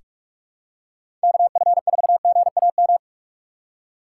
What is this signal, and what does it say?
Morse code, heard as a keyed tone, about 46 wpm, 700 Hz. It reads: KF4GAM
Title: USA Call Signs - 6 Characters - 3x Character Spacing - 45wpm